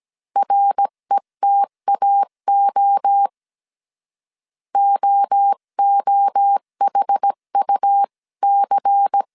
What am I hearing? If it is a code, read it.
Morse code, heard as a keyed tone, about 17 wpm, 800 Hz. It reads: RETAO OOHUC
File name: CW / DET / 1 6 1500C S C 16wpm 2733